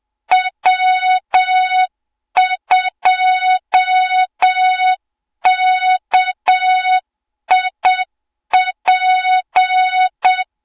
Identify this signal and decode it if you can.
Morse code, heard as a keyed tone, about 7 wpm, 3700 Hz, with noise. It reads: W2KIP
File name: Morze-HA2MN / hang / 20 GYAKORLAT